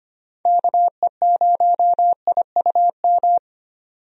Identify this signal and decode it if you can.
Morse code, heard as a keyed tone, about 25 wpm, 700 Hz. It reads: KE0IUM
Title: USA Call Signs - 6 Characters - 25wpm